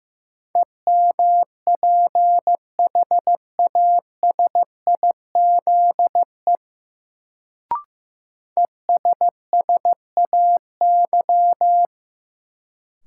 Morse code, heard as a keyed tone, about 16 wpm, 700 Hz. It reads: EMPHASIZE ESSAY
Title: Sentences from Top 1500 Words - New Word Drill - 15wpm